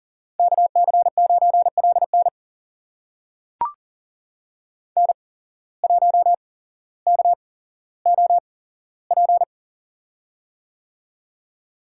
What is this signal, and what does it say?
Morse code, heard as a keyed tone, about 40 wpm, spaced as for 17 wpm, 700 Hz. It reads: KC9LN N1KOP
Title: USA Call Signs - 8x Character Spacing - 40wpm